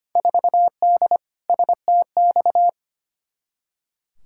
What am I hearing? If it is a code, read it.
Morse code, heard as a keyed tone, about 25 wpm, 700 Hz. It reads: 4D STX
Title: Field Day Exchanges Encore - 25wpm